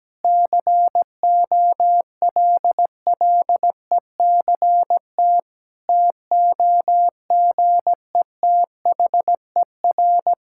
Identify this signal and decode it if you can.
Morse code, heard as a keyed tone, about 17 wpm, 700 Hz. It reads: COLLECT TOGETHER
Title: Sentences from Top 1600 Words 17wpm